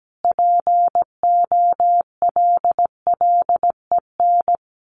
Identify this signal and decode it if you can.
Morse code, heard as a keyed tone, about 17 wpm, 700 Hz. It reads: POLLEN
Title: A Sound Of Thunder 17wpm